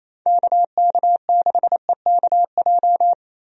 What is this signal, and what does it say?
Morse code, heard as a keyed tone, about 28 wpm, 700 Hz. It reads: KK6EKJ
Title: USA Call Signs - 6 Characters - Repeated 3x - 28wpm